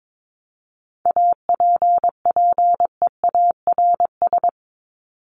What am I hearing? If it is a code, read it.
Morse code, heard as a keyed tone, about 22 wpm, 700 Hz. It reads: APPEARS
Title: ES:S Suffix - Top 100 Words with Base Words 22wpm